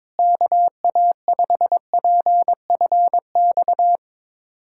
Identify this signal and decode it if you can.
Morse code, heard as a keyed tone, about 22 wpm, 700 Hz. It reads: KA5PFX
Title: USA Callsign DE Callsign - 6 Characters - 22wpm